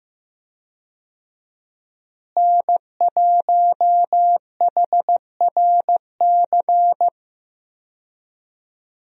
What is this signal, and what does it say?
Morse code, heard as a keyed tone, about 15 wpm, 700 Hz. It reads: N1HRC
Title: USA Call Signs - 5 Characters - Repeated 3x - 15wpm